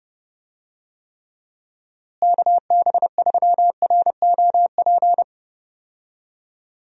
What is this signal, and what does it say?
Morse code, heard as a keyed tone, about 30 wpm, 700 Hz. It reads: KB3ROP
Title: USA Call Signs - 6 Characters - Repeated 3x - 30wpm